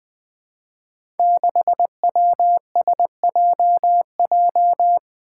Morse code, heard as a keyed tone, about 20 wpm, 700 Hz. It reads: 6WSJJ
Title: Five Challenging Letter-Number Combinations 20wpm